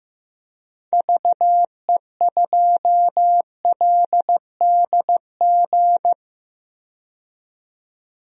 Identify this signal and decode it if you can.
Morse code, heard as a keyed tone, about 15 wpm, 700 Hz. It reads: VE2LDG
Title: DXCC-100 Call Signs - 6 Characters - 15wpm